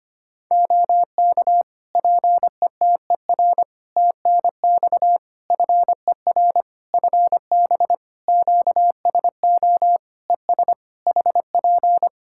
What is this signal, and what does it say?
Morse code, heard as a keyed tone, about 25 wpm, 700 Hz. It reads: OK PETER TNX FER FB QSO ES HP
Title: K1USN SST QSO Practice 25wpm